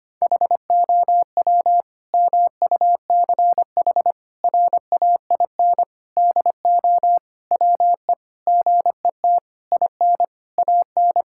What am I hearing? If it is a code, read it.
Morse code, heard as a keyed tone, about 25 wpm, 700 Hz. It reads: HOW MUCH RAIN DO WE GET IN AN
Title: Sentences from Top 1000 Words - Review Part 6 - 25wpm